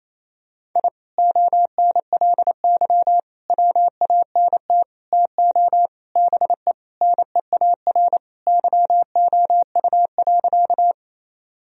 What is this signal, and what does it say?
Morse code, heard as a keyed tone, about 28 wpm, 700 Hz. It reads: I ONLY WANT TO BE NEAR YOU.
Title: Sentences from Top 1000 Words - Review Part 1 - 28wpm